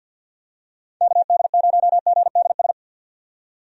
Morse code, heard as a keyed tone, about 51 wpm, 700 Hz. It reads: KD0GDS
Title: USA Call Signs - 6 Characters - 50wpm